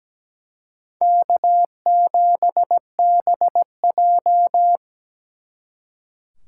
Morse code, heard as a keyed tone, about 20 wpm, 700 Hz. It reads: K7BJ
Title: Speed Racing - USA 1x2 Call Signs 17wpm